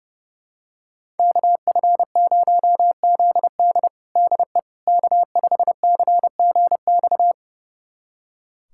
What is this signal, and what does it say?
Morse code, heard as a keyed tone, about 30 wpm, 700 Hz. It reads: KF0ZD DE K5CGX
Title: USA Callsign DE Callsign - 5 Characters - 30wpm